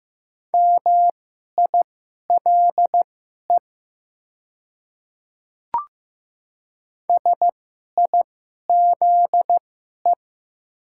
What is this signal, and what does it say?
Morse code, heard as a keyed tone, about 15 wpm, spaced as for 11 wpm, 700 Hz. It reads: MILE SIZE
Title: 2x Character Spacing - Top 500 Words - 15wpm